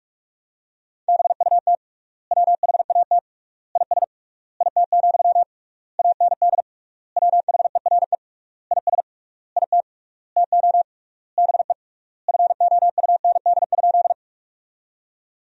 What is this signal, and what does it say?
Morse code, heard as a keyed tone, about 46 wpm, 700 Hz. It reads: BUT WHAT IS IT, AND WHERE IS IT TO BE FOUND?
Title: 3x Word-Spacing - Sentences from Top 300 Words - 45wpm